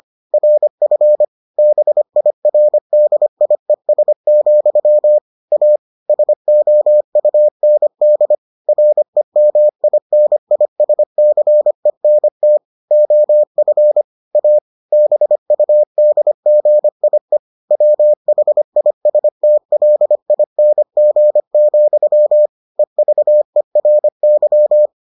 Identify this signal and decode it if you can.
Morse code, heard as a keyed tone, about 25 wpm, 600 Hz. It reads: RF BIRDIES, A SOUND REMINISCENT OF A BUDGIE WHISTLING, EVERY